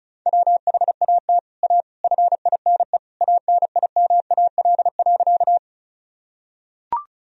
Morse code, heard as a keyed tone, about 34 wpm, 700 Hz. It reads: WHAT A FINE ANIMAL.
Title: Sentences from Top 1000 Words - Review Part 1 - 35wpm